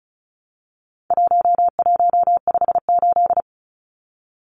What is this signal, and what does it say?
Morse code, heard as a keyed tone, about 34 wpm, 700 Hz. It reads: 1158
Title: Numbers - 4-Digit 35wpm